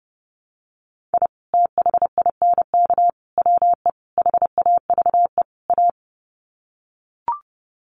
Morse code, heard as a keyed tone, about 28 wpm, 700 Hz. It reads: I THINK WE HAVE A
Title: Sets of 5 Words Encore 30wpm